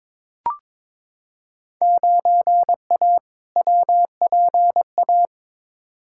Morse code, heard as a keyed tone, about 22 wpm, 700 Hz. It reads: 9A WPA
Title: Field Day Exchanges Encore - 22wpm